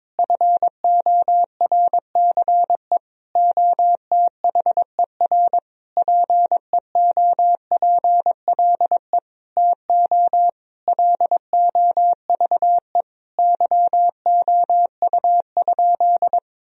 Morse code, heard as a keyed tone, about 21 wpm, 700 Hz. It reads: FORCE OTHER PEOPLE TO LOVE YOU?
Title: Sentences from Top 1000 Words - Review Part 3 - 22wpm